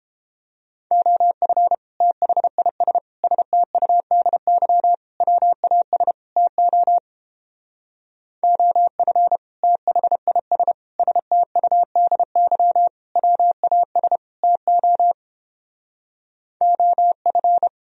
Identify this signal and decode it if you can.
Morse code, heard as a keyed tone, about 29 wpm, 700 Hz. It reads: OF THIS STUDY WAS TO OF THIS STUDY WAS TO OF